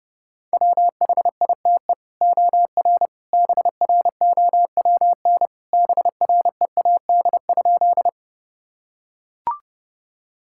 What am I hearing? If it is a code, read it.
Morse code, heard as a keyed tone, about 30 wpm, 700 Hz. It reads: WHITE OR BROWN BREAD?
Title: Sentences from Top 1600 Words 30wpm